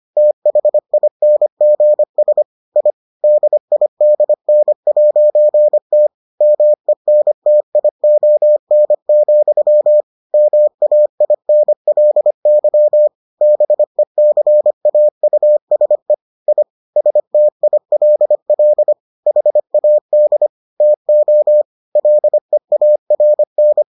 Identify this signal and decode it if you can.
Morse code, heard as a keyed tone, about 25 wpm, 600 Hz. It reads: THINGS I DIDN'T MENTION, MAINLY BECAUSE I STILL HAD TO LEARN